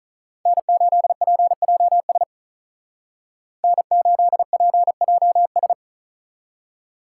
Morse code, heard as a keyed tone, about 46 wpm, 700 Hz. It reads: N8PJS N8PJS